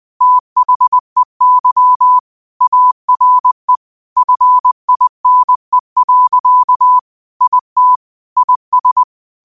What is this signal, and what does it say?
Morse code, heard as a keyed tone, about 20 wpm, 1000 Hz. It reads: THEY ARE FINE. IT IS